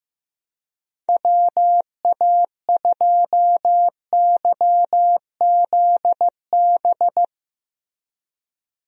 Morse code, heard as a keyed tone, about 16 wpm, 700 Hz. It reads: WA2YZB
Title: USA Call Signs - 6 Characters - Repeated 3x - 15wpm